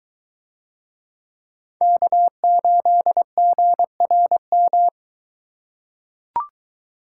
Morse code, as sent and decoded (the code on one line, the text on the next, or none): -.- ---.. --. .-. --
K8GRM